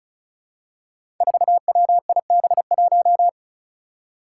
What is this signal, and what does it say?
Morse code, heard as a keyed tone, about 36 wpm, 700 Hz. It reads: 4WIB1
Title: Five Challenging Letter-Number Combinations 35wpm